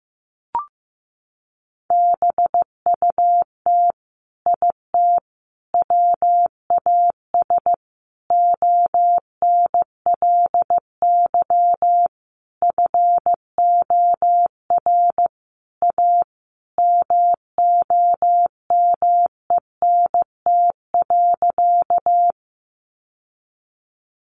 Morse code, heard as a keyed tone, about 15 wpm, 700 Hz. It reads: BUT IT WAS ONLY FOR A MOMENT.
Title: Sentences from Top 1000 Words 15wpm